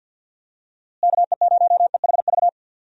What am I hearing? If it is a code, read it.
Morse code, heard as a keyed tone, about 46 wpm, 700 Hz. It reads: KE0EHV
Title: USA Call Signs - 6 Characters - Repeated 2x - 50wpm